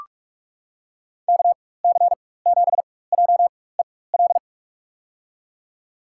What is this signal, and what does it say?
Morse code, heard as a keyed tone, about 45 wpm, spaced as for 21 wpm, 700 Hz. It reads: KC7JEL
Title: USA Call Signs - 6 Characters - 4x Character Spacing - 45wpm